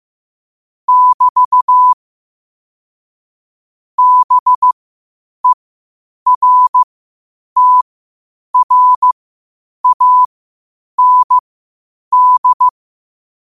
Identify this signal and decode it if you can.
Morse code, heard as a keyed tone, about 15 wpm, spaced as for 8 wpm, 1000 Hz. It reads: <BT> BERTRAND
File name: qotdfiles / 8wpm